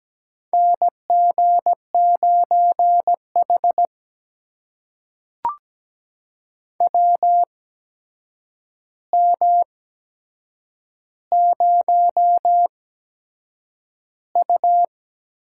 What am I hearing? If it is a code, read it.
Morse code, heard as a keyed tone, about 17 wpm, spaced as for 7 wpm, 700 Hz. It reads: NG9H WM0U
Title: USA Call Signs - 4 Characters - 8x Character Spacing - 17wpm